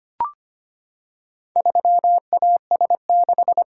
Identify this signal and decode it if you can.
Morse code, heard as a keyed tone, about 25 wpm, 700 Hz. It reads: 3AS6
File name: practice / 25wpm / Four Letter-Number Combinations - 25wpm